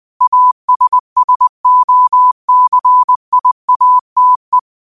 Morse code, heard as a keyed tone, about 20 wpm, 1000 Hz. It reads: ASSOCIATE
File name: qotdfiles / 20wpm